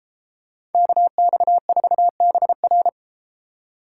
Morse code, heard as a keyed tone, about 30 wpm, 700 Hz. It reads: KX4BR